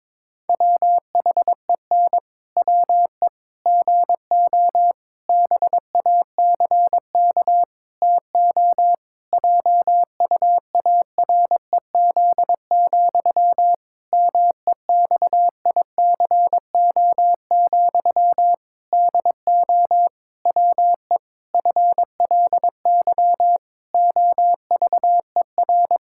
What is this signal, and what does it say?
Morse code, heard as a keyed tone, about 22 wpm, 700 Hz. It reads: WHEN WE GO BACK TO JUAREZ, MEXICO, DO WE FLY OVER
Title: Pangram Challenge v2 - 22wpm